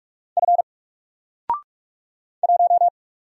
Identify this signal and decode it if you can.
Morse code, heard as a keyed tone, about 44 wpm, 700 Hz. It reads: F 1
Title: Single Letter-Number Alt 45wpm